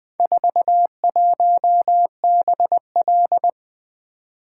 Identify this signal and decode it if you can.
Morse code, heard as a keyed tone, about 20 wpm, 700 Hz. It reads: 41BL
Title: Four Letter-Number Combinations - 20wpm